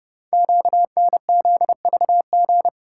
Morse code, heard as a keyed tone, about 30 wpm, 700 Hz. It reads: QNZVG
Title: Code Groups - 5 Letters - 30wpm